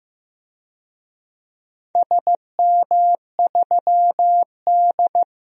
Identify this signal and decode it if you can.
Morse code, heard as a keyed tone, about 15 wpm, 700 Hz. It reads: SM3D
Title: DXCC-100 Call Signs - 4 Characters - 15wpm